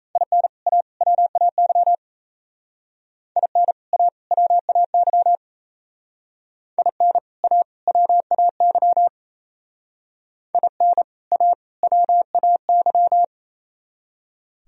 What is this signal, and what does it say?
Morse code, heard as a keyed tone, about 36 wpm, 700 Hz. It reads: IN A WAY IN A WAY IN A WAY IN A WAY